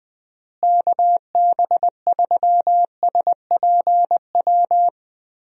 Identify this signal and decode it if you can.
Morse code, heard as a keyed tone, about 19 wpm, 700 Hz. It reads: KB3SPW
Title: USA Call Signs - 6 Characters - Repeated 3x - 20wpm